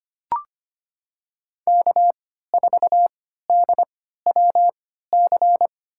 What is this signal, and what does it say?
Morse code, heard as a keyed tone, about 25 wpm, spaced as for 14 wpm, 700 Hz. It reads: K4DWC
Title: USA Call Signs - 3x Character Spacing - 25wpm